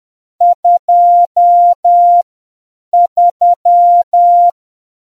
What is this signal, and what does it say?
Morse code, heard as a keyed tone, about 10 wpm, spaced as for 7 wpm, 700 Hz. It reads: 23